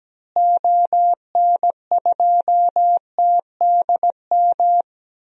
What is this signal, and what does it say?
Morse code, heard as a keyed tone, about 17 wpm, 700 Hz. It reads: ON2TDM
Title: DXCC-100 Call Signs - 6 Characters - Repeated 2x - 17wpm